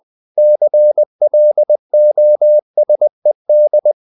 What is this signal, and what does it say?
Morse code, heard as a keyed tone, about 20 wpm, 600 Hz. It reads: CLOSED
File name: the old man and the sea part 2 20wpm